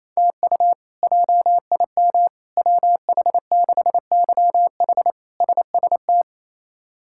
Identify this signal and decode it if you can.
Morse code, heard as a keyed tone, about 28 wpm, 700 Hz. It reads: TU JIM WH6YH SST
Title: K1USN SST QSO Practice - 28wpm